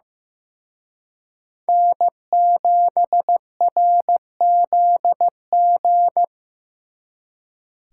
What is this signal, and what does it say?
Morse code, heard as a keyed tone, about 18 wpm, 700 Hz. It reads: N7RZG
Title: Speed Racing - USA 1x3 Call Signs 15wpm